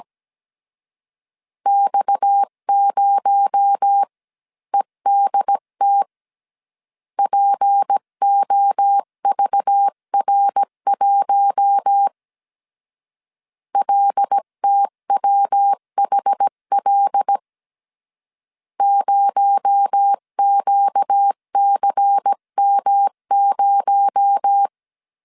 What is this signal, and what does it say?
Morse code, heard as a keyed tone, about 17 wpm, 800 Hz. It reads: X0EDT POVR1 LTWHL 0QCM0
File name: CW / DET / 1 6 1500C S C 16wpm 2733